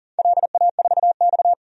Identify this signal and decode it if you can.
Morse code, heard as a keyed tone, about 40 wpm, 700 Hz. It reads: LA4X
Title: DXCC-100 Call Signs - 4 Characters - 40wpm